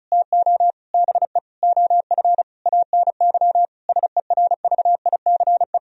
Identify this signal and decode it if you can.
Morse code, heard as a keyed tone, about 36 wpm, 700 Hz. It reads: TO BE OF ANY SERVICE
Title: Sentences from Top 1000 Words - Review Part 4 - 35wpm